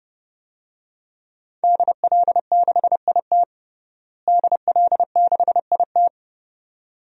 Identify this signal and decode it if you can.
Morse code, heard as a keyed tone, about 32 wpm, 700 Hz. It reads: DL6IT DL6IT
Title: DXCC-100 Call Signs - 5 Characters - Repeated 2x - 30wpm